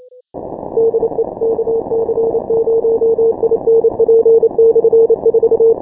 Morse code, heard as a keyed tone, about 28 wpm, 470 Hz. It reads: DE KC0INP/4
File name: final / W1UJ-168-140-MorseRunner